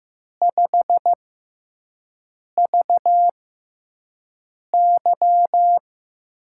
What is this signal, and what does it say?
Morse code, heard as a keyed tone, about 15 wpm, spaced as for 5 wpm, 700 Hz. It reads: 5VY
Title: USA Call Signs - 6 Characters - 6x Character Spacing - 15wpm